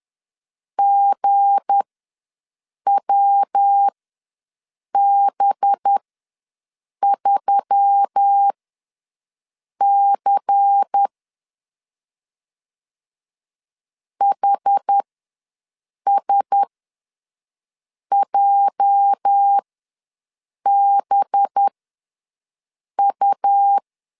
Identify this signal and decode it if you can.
Morse code, heard as a keyed tone, about 10 wpm, spaced as for 5 wpm, 800 Hz. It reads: GWB3C HSJBU
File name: LEZIONI / 8 YJ45 / 8 YJ ws 10wpm 4043